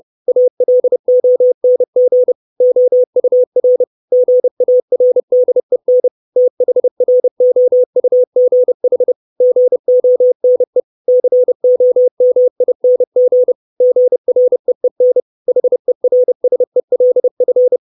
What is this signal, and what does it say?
Morse code, heard as a keyed tone, about 30 wpm, 500 Hz. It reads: ALONG OUR GARDEN THROUGH GONE COMING GREEN HERSELF
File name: 632 Words 30WPM